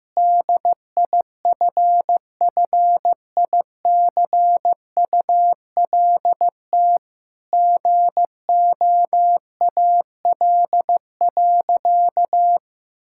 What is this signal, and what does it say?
Morse code, heard as a keyed tone, about 15 wpm, 700 Hz. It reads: DIFFICULT GOAL.
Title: Sentences from Top 1000 Words - Review Part 5 - 15wpm